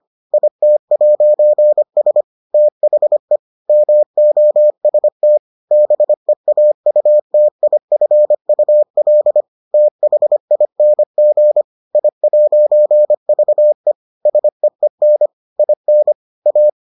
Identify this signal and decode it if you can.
Morse code, heard as a keyed tone, about 25 wpm, 600 Hz. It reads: IT'S THE MOST BEAUTIFUL THING I'VE SEEN IN A